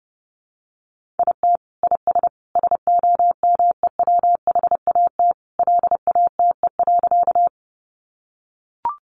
Morse code, heard as a keyed tone, about 30 wpm, 700 Hz. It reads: IT IS SOMEWHAT LATE.
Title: Sentences from Top 1300 Words 30wpm